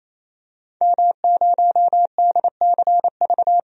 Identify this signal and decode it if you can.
Morse code, heard as a keyed tone, about 28 wpm, 700 Hz. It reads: M0DCV
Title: DXCC-100 Call Signs - 5 Characters - 28wpm